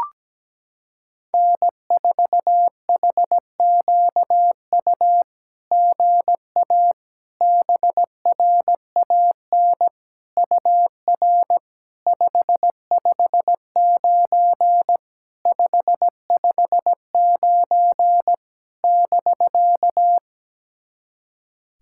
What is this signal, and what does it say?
Morse code, heard as a keyed tone, about 17 wpm, 700 Hz. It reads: N4HQU GA BRAN UR 559 559 <BK>
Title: SOTA QSO Practice 17wpm